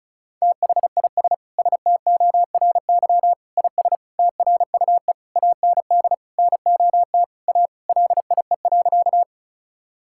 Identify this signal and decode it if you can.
Morse code, heard as a keyed tone, about 36 wpm, 700 Hz. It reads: THIS STORY IS TRUE AND NOT A LIE.
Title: Sentences from Top 1600 Words 35wpm